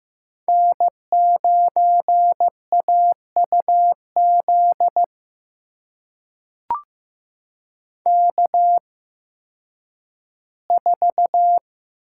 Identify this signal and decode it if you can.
Morse code, heard as a keyed tone, about 15 wpm, spaced as for 10 wpm, 700 Hz. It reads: N9AUZ K4
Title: USA Call Signs - 5 Characters - 8x Character Spacing - 15wpm